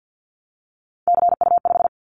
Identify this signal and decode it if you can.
Morse code, heard as a keyed tone, about 50 wpm, 700 Hz. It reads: CU5
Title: Callsign Prefixes for Top 50 DXCC - 50wpm